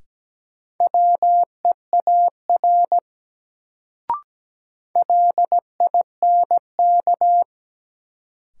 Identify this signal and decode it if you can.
Morse code, heard as a keyed tone, about 17 wpm, 700 Hz. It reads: WEAR LINK
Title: Sentences from Top 1500 Words - New Word Drill Alt - 17wpm